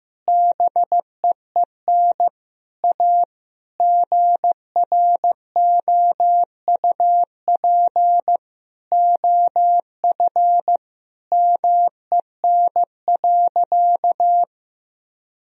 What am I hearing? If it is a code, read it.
Morse code, heard as a keyed tone, about 16 wpm, 700 Hz. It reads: BEEN A GROUP OF MEN.
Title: Sentences from Top 1000 Words - Review Part 1 - 15wpm